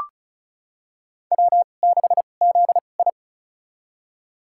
Morse code, heard as a keyed tone, about 34 wpm, spaced as for 25 wpm, 700 Hz. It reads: W6ZI
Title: USA Call Signs - 4 Characters - 2x Character Spacing - 35wpm